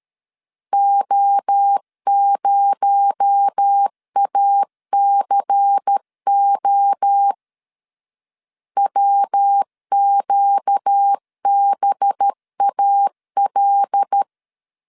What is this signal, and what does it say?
Morse code, heard as a keyed tone, about 13 wpm, 800 Hz. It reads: O0ACO WQBAL
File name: CW / DET / 1 6 1500C S B 12wpm 3431